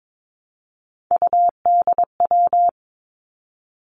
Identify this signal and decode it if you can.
Morse code, heard as a keyed tone, about 22 wpm, 700 Hz. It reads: UDW
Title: Three Challenging Letter-Number Combinations 22wpm